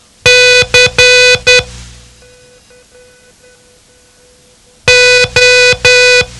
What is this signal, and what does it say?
Morse code, heard as a keyed tone, about 10 wpm, 500 Hz. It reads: C O
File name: ham radio / CodeQuick / Code Quick 07